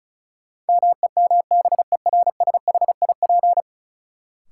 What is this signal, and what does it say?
Morse code, heard as a keyed tone, about 34 wpm, 700 Hz. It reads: MEMBERSHIP
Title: Sentences from Top 1400 Words - New Word Drill - 35wpm